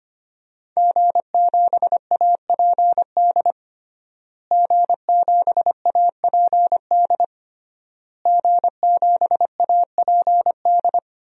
Morse code, heard as a keyed tone, about 25 wpm, 700 Hz. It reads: G7APD G7APD G7APD
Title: DXCC-100 Call Signs - 5 Characters - Repeated 3x - 25wpm